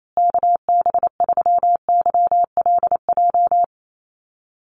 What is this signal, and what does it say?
Morse code, heard as a keyed tone, about 28 wpm, 690 Hz. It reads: KB3YLJ
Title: USA Call Signs Repeated 3x 28wpm